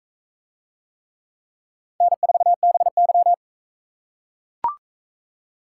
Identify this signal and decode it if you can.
Morse code, heard as a keyed tone, about 42 wpm, 700 Hz. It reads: N4BY